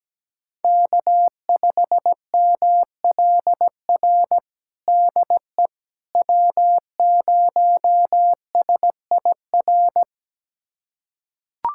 Morse code, heard as a keyed tone, about 17 wpm, 700 Hz. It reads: K5MLR DE W0SIR
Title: USA Callsign DE Callsign - 5 Characters - 17wpm